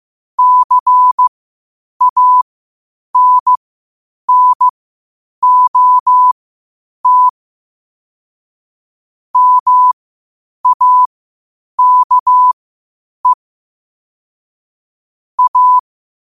Morse code, heard as a keyed tone, about 15 wpm, spaced as for 8 wpm, 1000 Hz. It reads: CANNOT MAKE A
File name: qotdfiles / 8wpm